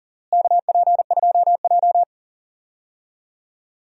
Morse code, heard as a keyed tone, about 37 wpm, 700 Hz. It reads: KP2J
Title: DXCC-100 Call Signs - 4 Characters - Repeated 2x - 40wpm